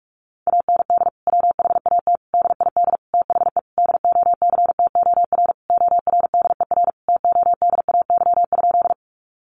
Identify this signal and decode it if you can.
Morse code, heard as a keyed tone, about 44 wpm, 700 Hz. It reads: AND WHAT DID THE DOCTOR ORDER TODAY?
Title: Sentences from Top 1500 Words 45wpm